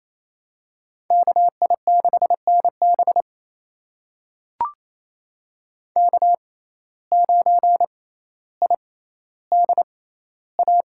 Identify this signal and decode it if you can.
Morse code, heard as a keyed tone, about 28 wpm, spaced as for 14 wpm, 700 Hz. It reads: KI6NB K9IDA
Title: USA Call Signs - 5 Characters - 6x Character Spacing - 28wpm